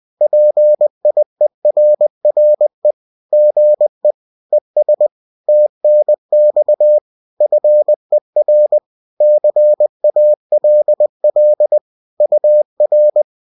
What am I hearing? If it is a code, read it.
Morse code, heard as a keyed tone, about 20 wpm, 600 Hz. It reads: PIERRE GE ES TNX FER CALL UR